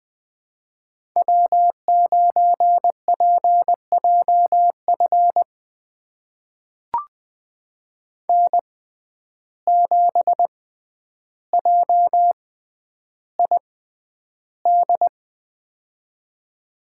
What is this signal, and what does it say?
Morse code, heard as a keyed tone, about 20 wpm, spaced as for 10 wpm, 700 Hz. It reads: W9PJF N7JID
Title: USA Call Signs - 5 Characters - 6x Character Spacing - 20wpm